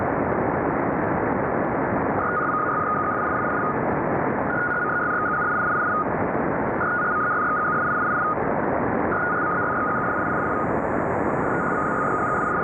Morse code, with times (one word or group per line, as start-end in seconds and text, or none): none